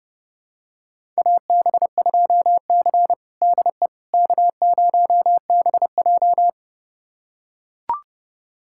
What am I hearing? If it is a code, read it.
Morse code, heard as a keyed tone, about 32 wpm, 700 Hz. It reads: AB2C DE K0BJ
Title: USA Callsign DE Callsign - 4 Characters - 30wpm